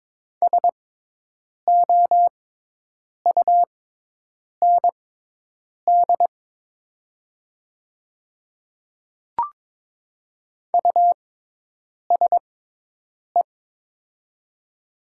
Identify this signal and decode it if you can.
Morse code, heard as a keyed tone, about 22 wpm, spaced as for 8 wpm, 700 Hz. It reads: SOUND USE
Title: 6x Character Spacing - Top 100 Words - 22wpm